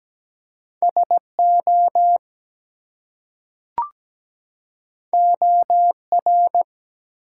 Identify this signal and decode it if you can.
Morse code, heard as a keyed tone, about 17 wpm, 700 Hz. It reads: SO OR
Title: Two Letter Words - 17wpm